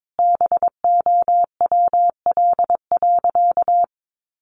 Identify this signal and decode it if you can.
Morse code, heard as a keyed tone, about 22 wpm, 700 Hz. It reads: BOWL.
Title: Sentences from Top 1200 Words 22wpm